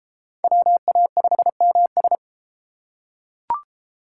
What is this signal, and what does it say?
Morse code, heard as a keyed tone, about 29 wpm, 700 Hz. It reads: WA5MS